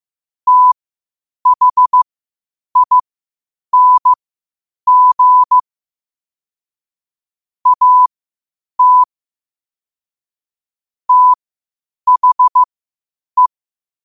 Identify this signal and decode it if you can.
Morse code, heard as a keyed tone, about 15 wpm, spaced as for 8 wpm, 1000 Hz. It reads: THING AT THE
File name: qotdfiles / 8wpm